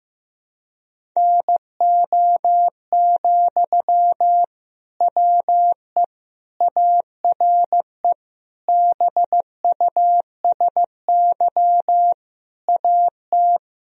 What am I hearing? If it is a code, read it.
Morse code, heard as a keyed tone, about 16 wpm, 700 Hz. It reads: NO, WE ARE BUSY AT
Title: Sentences from Top 1000 Words - Review Part 3 - 15wpm